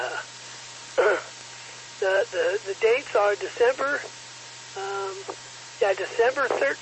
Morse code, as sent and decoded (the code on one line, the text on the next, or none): none